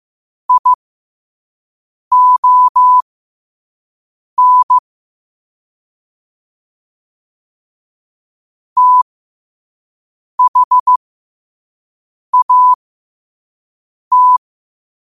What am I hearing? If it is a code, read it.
Morse code, heard as a keyed tone, about 15 wpm, spaced as for 5 wpm, 1000 Hz. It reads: ION THAT